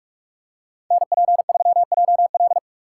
Morse code, heard as a keyed tone, about 45 wpm, 700 Hz. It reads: NP3JL